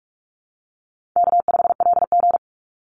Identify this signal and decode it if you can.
Morse code, heard as a keyed tone, about 44 wpm, 700 Hz. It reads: K5LG
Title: Call Signs Repeated 3x - 45wpm